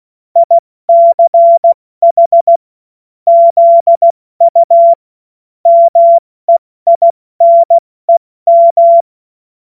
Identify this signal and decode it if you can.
Morse code, heard as a keyed tone, about 16 wpm, 680 Hz. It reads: ICH ZU MEINEM
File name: Morsekurs / Witze / Witz 18. Mai 2021, 70 F80